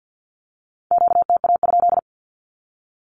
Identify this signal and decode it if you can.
Morse code, heard as a keyed tone, about 48 wpm, 700 Hz. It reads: QTA?
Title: Q-Codes 50wpm